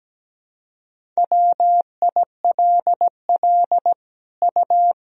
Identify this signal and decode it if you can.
Morse code, heard as a keyed tone, about 17 wpm, 700 Hz. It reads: WILL U
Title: Sentences from Top 200 Words and Abbreviations 17wpm